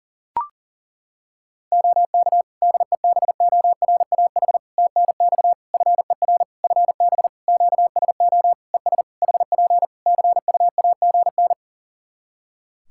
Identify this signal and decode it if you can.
Morse code, heard as a keyed tone, about 40 wpm, 700 Hz. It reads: OK DEBORAH TNX FER FB QSO ES HP CUAGN
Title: Practice QSO - Exchange 4 - Ending - 40wpm